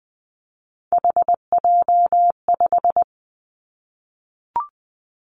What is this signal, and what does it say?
Morse code, heard as a keyed tone, about 20 wpm, 700 Hz. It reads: HJ5
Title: Callsign Prefixes for Top 100 DXCC - 20wpm